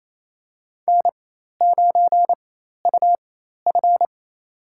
Morse code, heard as a keyed tone, about 28 wpm, spaced as for 13 wpm, 700 Hz. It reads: N9UF
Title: USA Call Signs - 4x Character Spacing - 28wpm